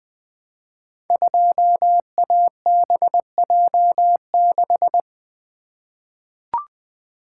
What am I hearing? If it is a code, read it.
Morse code, heard as a keyed tone, about 20 wpm, 700 Hz. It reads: 2ABJ6
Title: Code Groups - 5 Letter:Numbers - 20wpm